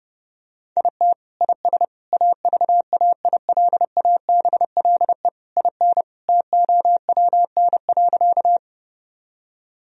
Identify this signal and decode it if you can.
Morse code, heard as a keyed tone, about 30 wpm, 700 Hz. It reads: IT IS AVAILABLE IN TOWN.
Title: Sentences from Top 1000 Words - Review Part 3 - 30wpm